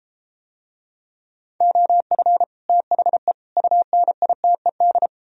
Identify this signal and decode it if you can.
Morse code, heard as a keyed tone, about 32 wpm, 700 Hz. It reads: OF THE UNITED